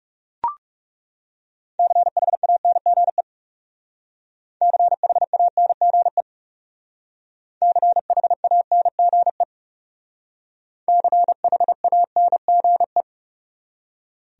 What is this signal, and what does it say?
Morse code, heard as a keyed tone, about 38 wpm, 700 Hz. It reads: CHANGE CHANGE CHANGE CHANGE